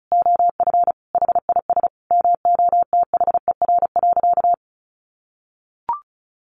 Morse code, heard as a keyed tone, about 34 wpm, 700 Hz. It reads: OF HIS MOTHER.
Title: Sentences from Top 1300 Words 35wpm